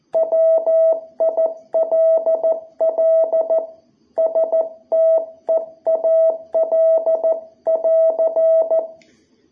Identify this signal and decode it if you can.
Morse code, heard as a keyed tone, about 14 wpm, 660 Hz. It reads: WILL STEAL <AR>
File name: jh8cbh / 2021cw